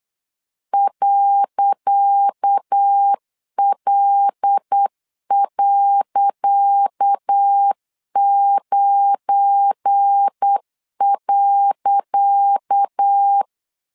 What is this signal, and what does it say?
Morse code, heard as a keyed tone, about 8 wpm, 800 Hz. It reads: .L.9.
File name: LEZIONI / 10 SIGNS / 10C DOT 8wpm 3649